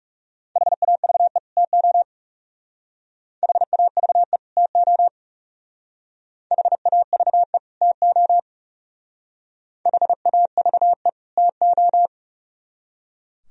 Morse code, heard as a keyed tone, about 34 wpm, 700 Hz. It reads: HAVE TO HAVE TO HAVE TO HAVE TO
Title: Speed-Racing - Sets of 2 Words 30wpm